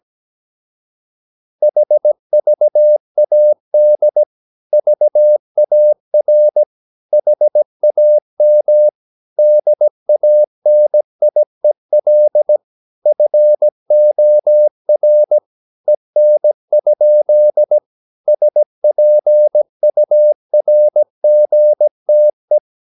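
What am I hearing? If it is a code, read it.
Morse code, heard as a keyed tone, about 17 wpm, 600 Hz. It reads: HVAD VAR HAM DANIEL FOR EN? SPURGTE
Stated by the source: Slow (QRS) Morse practice at 17 wpm, from CW med Gnister.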